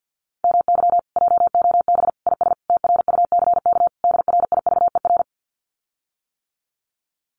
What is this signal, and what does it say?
Morse code, heard as a keyed tone, about 50 wpm, 700 Hz. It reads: MY JOB IS TRUCK DRIVER
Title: Common QSO Phrases Encore 50wpm